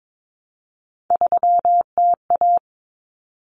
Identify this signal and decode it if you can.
Morse code, heard as a keyed tone, about 22 wpm, 700 Hz. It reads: 3TA
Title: Common Cut Numbers - 3-Digit 22wpm